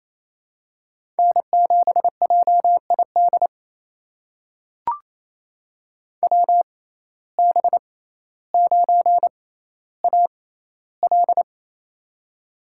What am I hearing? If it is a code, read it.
Morse code, heard as a keyed tone, about 28 wpm, spaced as for 14 wpm, 700 Hz. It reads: N7JID WB9AL
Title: USA Call Signs - 6x Character Spacing - 28wpm